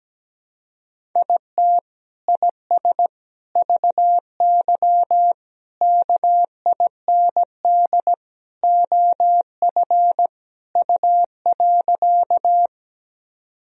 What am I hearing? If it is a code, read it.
Morse code, heard as a keyed tone, about 17 wpm, 700 Hz. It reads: IT IS VY KIND OF U.
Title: Sentences from Top 200 Words and Abbreviations 17wpm